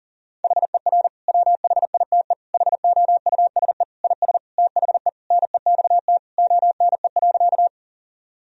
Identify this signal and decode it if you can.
Morse code, heard as a keyed tone, about 40 wpm, 700 Hz. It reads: HER WHITE HOUSE IS THE NEXT ONE.
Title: Sentences from Top 1000 Words - Review Part 1 - 40wpm